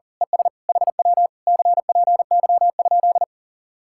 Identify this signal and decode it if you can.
Morse code, heard as a keyed tone, about 38 wpm, 700 Hz. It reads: ES HW CPY?
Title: Speed-Racing Common QSO Phrases - 40wpm